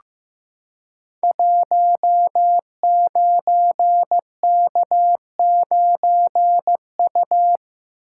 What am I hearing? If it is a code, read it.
Morse code, heard as a keyed tone, about 15 wpm, 700 Hz. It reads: 19K9U
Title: Code Groups - 5 Letter:Numbers - 15wpm